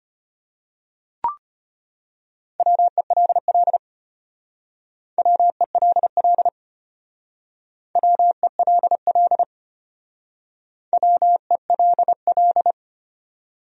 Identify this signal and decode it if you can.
Morse code, heard as a keyed tone, about 32 wpm, 700 Hz. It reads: WELL WELL WELL WELL